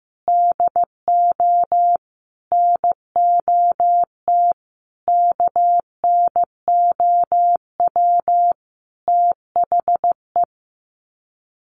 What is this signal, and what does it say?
Morse code, heard as a keyed tone, about 15 wpm, 700 Hz. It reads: DO NOT KNOW THE
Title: Sets of 5 Words 15wpm